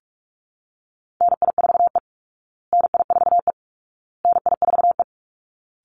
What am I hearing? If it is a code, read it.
Morse code, heard as a keyed tone, about 44 wpm, 700 Hz. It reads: NI4E NI4E NI4E
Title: Call Signs Repeated 2x - 45wpm